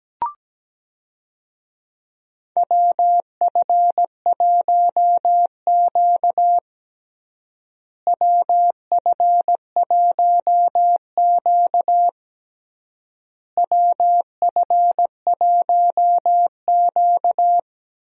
Morse code, heard as a keyed tone, about 17 wpm, 700 Hz. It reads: WF1Q WF1Q WF1Q
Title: USA Call Signs - 4 Characters - Repeated 3x - 17wpm